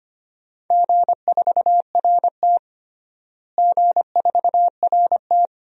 Morse code, heard as a keyed tone, about 25 wpm, 700 Hz. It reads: G4RT G4RT
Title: DXCC-100 Call Signs - 4 Characters - Repeated 2x - 25wpm